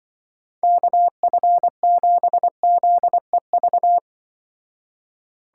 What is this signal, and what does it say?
Morse code, heard as a keyed tone, about 25 wpm, 700 Hz. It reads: KF7ZEV
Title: Speed Racing - USA 2x3 Call Signs 20wpm